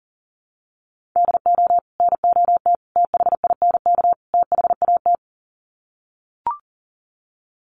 Morse code, heard as a keyed tone, about 40 wpm, 700 Hz. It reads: DO NOT THINK THAT
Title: Sets of 4 Words 40wpm